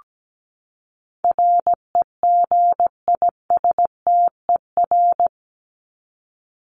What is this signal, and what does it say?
Morse code, heard as a keyed tone, about 17 wpm, 700 Hz. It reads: REGISTER
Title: Top 1001-1250 Words 17wpm